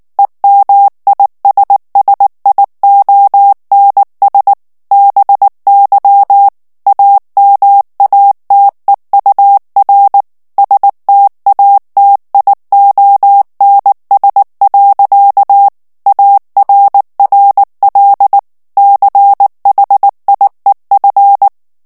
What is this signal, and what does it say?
Morse code, heard as a keyed tone, about 19 wpm, 800 Hz. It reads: EMISSIONS BY AMATEUR STATIONS. ARRL CHIEF
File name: ANGLAIS texte clair 3 vitesse 18 mots minute